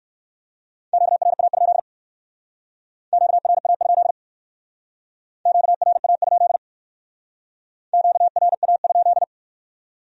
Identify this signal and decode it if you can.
Morse code, heard as a keyed tone, about 64 wpm, 700 Hz. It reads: QRA? QRA? QRA? QRA?